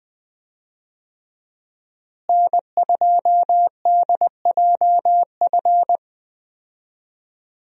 Morse code, heard as a keyed tone, about 20 wpm, 700 Hz. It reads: N2DJF
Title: USA Call Signs - 5 Characters - Repeated 3x - 20wpm